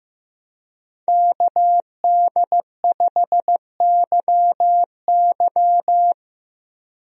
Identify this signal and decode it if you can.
Morse code, heard as a keyed tone, about 15 wpm, 700 Hz. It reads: KD5YY
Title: USA Call Signs - 5 Characters - Repeated 2x - 15wpm